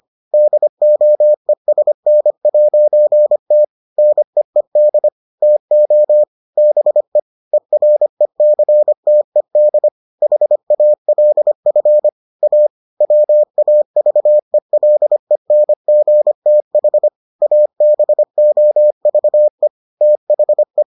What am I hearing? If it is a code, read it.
Morse code, heard as a keyed tone, about 25 wpm, 600 Hz. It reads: DOESN'T NEED TO BE ERECTED HALF A WAVELENGTH ABOVE THE